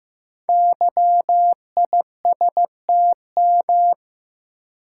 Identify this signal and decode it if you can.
Morse code, heard as a keyed tone, about 15 wpm, 700 Hz. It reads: YISTM
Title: Code Groups - 5 Letters - 15wpm